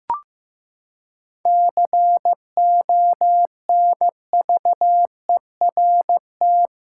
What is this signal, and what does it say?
Morse code, heard as a keyed tone, about 15 wpm, 700 Hz. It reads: CONVERT
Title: Seven Letter Words - 15wpm